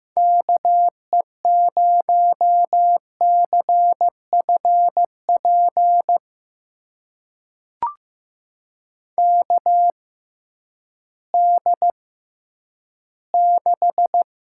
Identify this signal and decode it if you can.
Morse code, heard as a keyed tone, about 15 wpm, spaced as for 10 wpm, 700 Hz. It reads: KE0CFP KD6
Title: USA Call Signs - 6 Characters - 6x Character Spacing - 15wpm